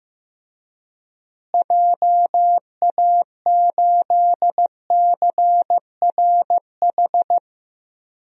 Morse code, heard as a keyed tone, about 15 wpm, 700 Hz. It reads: JA8CRH
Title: DXCC-100 Call Signs - 6 Characters - Repeated 3x - 15wpm